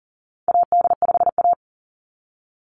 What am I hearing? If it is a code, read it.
Morse code, heard as a keyed tone, about 38 wpm, 700 Hz. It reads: AD5A
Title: NA SOTA Chaser Callsigns 40wpm